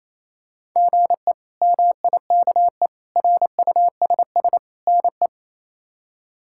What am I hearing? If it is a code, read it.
Morse code, heard as a keyed tone, about 28 wpm, 700 Hz. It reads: GE MIKE RUSS NE
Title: K1USN SST QSO Practice - 28wpm